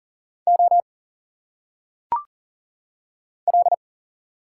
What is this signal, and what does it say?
Morse code, heard as a keyed tone, about 40 wpm, 700 Hz. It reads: O L
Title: Single Letters - Phonetic Pronunciation 40wpm